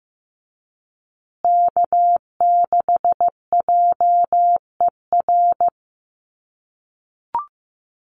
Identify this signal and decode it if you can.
Morse code, heard as a keyed tone, about 15 wpm, 700 Hz. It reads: K6JER
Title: USA Call Signs Repeated 3x 15wpm